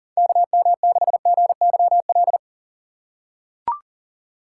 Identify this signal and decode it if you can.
Morse code, heard as a keyed tone, about 40 wpm, 700 Hz. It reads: KM6GYL
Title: USA Call Signs - 6 Characters - 6x Character Spacing - 40wpm